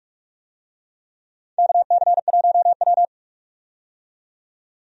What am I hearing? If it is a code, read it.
Morse code, heard as a keyed tone, about 44 wpm, 700 Hz. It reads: KC1W
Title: USA Call Signs - 4 Characters - 45wpm